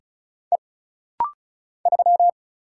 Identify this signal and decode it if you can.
Morse code, heard as a keyed tone, about 36 wpm, 700 Hz. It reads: E 3
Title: Single Letter-Number Alt - Rapid-Fire 35wpm